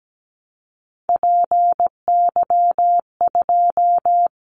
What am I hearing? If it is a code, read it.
Morse code, heard as a keyed tone, about 17 wpm, 700 Hz. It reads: PY2
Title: Callsign Prefixes for Top 100 DXCC - 17wpm